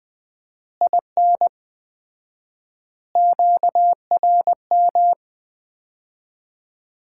Morse code, heard as a keyed tone, about 20 wpm, 700 Hz. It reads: IN QRM
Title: Common QSO Phrases - 4x Word-Spacing - 20wpm